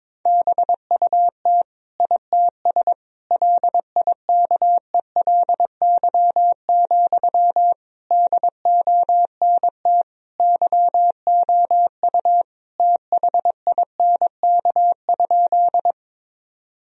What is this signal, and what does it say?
Morse code, heard as a keyed tone, about 22 wpm, 700 Hz. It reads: BUT ITS LIKELY, DONT YOU THINK?
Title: Sentences from Top 1000 Words - Review Part 4 - 22wpm